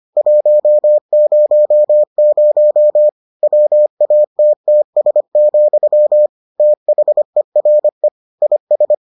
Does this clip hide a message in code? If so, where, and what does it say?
Yes: Morse code from 0.2 to 8.9 s, heard as a keyed tone, about 25 wpm, 600 Hz. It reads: 100 WATTS, THERE IS